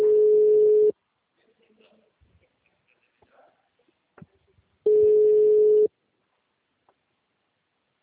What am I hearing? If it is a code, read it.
Ringback tone (Europe, 425 Hz), with speech.
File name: out- -2032- - - .34